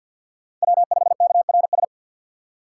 Morse code, heard as a keyed tone, about 53 wpm, 700 Hz. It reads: W5KRS
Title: USA Call Signs - 5 Characters - 3x Character Spacing - 50wpm